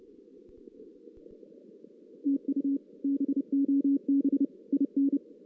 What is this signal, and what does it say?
Morse code, heard as a keyed tone, about 30 wpm, 280 Hz. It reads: TU BOB IN
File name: 2016 JAN CW NAQP